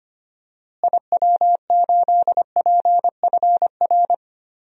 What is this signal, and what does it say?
Morse code, heard as a keyed tone, about 25 wpm, 700 Hz. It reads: IW8PFR
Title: DXCC-100 Call Signs - 6 Characters - Repeated 3x - 25wpm